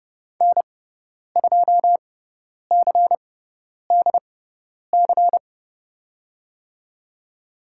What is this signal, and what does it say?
Morse code, heard as a keyed tone, about 30 wpm, spaced as for 10 wpm, 700 Hz. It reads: N2CDC
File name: Lesson 17g - Letter u - Call Signs 30wpm